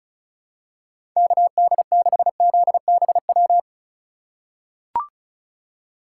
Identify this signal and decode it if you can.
Morse code, heard as a keyed tone, about 34 wpm, 700 Hz. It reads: KD6ZBW
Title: USA Call Signs - 6 Characters - 2x Character Spacing - 35wpm